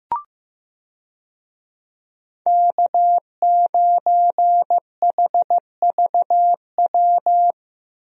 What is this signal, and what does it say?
Morse code, heard as a keyed tone, about 15 wpm, 700 Hz. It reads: K9HVW
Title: USA Call Signs - 5 Characters - Repeated 2x - 15wpm